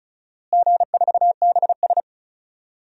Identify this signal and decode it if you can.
Morse code, heard as a keyed tone, about 36 wpm, 700 Hz. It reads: G4BS
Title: DXCC-100 Call Signs - 4 Characters - Repeated 2x - 35wpm